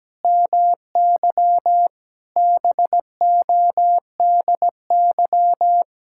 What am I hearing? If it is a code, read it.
Morse code, heard as a keyed tone, about 17 wpm, 700 Hz. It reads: MY BODY
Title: Sentences from Top 1000 Words - Review Part 6 - 17wpm